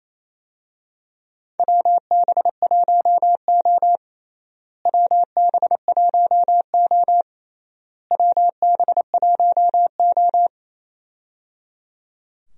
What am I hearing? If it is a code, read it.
Morse code, heard as a keyed tone, about 28 wpm, 700 Hz. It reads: WB1O WB1O WB1O
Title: USA Call Signs - 4 Characters - Repeated 3x - 28wpm